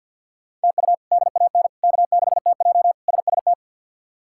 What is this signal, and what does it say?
Morse code, heard as a keyed tone, about 52 wpm, 700 Hz. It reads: TU DAN K6TJ SST
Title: K1USN SST QSO Practice - 50wpm